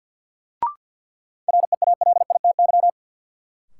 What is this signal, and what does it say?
Morse code, heard as a keyed tone, about 52 wpm, 700 Hz. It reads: REALITY
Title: Sentences from Top 1200 Words - New Word Drill Alt - 50wpm